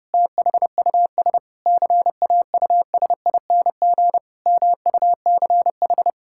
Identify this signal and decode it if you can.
Morse code, heard as a keyed tone, about 30 wpm, 700 Hz. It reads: THUS CAUSING MUCH
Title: Pangram Challenge v2 - 30wpm